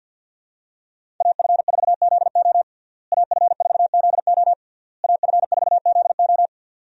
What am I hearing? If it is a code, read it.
Morse code, heard as a keyed tone, about 52 wpm, 700 Hz. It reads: AF4ZO AF4ZO AF4ZO
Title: USA Call Signs - 5 Characters - Repeated 3x - 50wpm